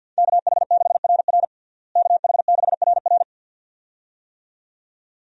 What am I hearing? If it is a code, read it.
Morse code, heard as a keyed tone, about 50 wpm, 700 Hz. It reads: KH6RR KH6RR
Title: USA Call Signs - 5 Characters - Repeated 2x - 50wpm